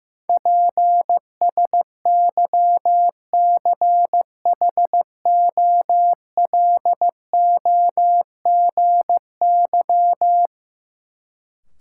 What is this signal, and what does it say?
Morse code, heard as a keyed tone, about 15 wpm, 700 Hz. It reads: PSYCHOLOGY
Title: Sentences from Top 1400 Words - New Word Drill Alt - 15wpm